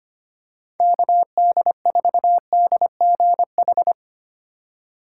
Morse code, heard as a keyed tone, about 25 wpm, 700 Hz. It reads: KD4DGH
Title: USA Call Signs - 8x Character Spacing - 25wpm